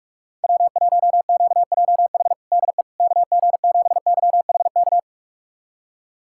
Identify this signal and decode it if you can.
Morse code, heard as a keyed tone, about 47 wpm, 700 Hz. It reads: W1QJH DE KG7YHK
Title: USA Callsign DE Callsign 45wpm